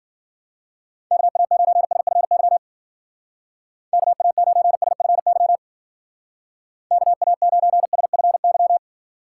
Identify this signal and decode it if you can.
Morse code, heard as a keyed tone, about 52 wpm, 700 Hz. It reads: KA9SFY KA9SFY KA9SFY